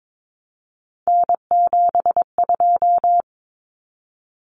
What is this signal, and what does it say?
Morse code, heard as a keyed tone, about 22 wpm, 700 Hz. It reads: N72
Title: Common Cut Numbers - 3-Digit 22wpm